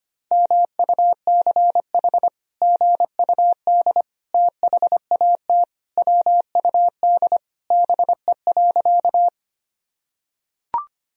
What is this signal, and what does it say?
Morse code, heard as a keyed tone, about 25 wpm, 700 Hz. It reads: MUCH GUD THAT WUD BE.
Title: Sentences from Top 200 Words and Abbreviations 25wpm